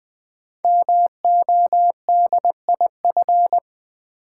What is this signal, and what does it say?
Morse code, heard as a keyed tone, about 20 wpm, 700 Hz. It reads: MODIF
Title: Build A Word - 5 Letters At A Time - 12-Letter Words - 20wpm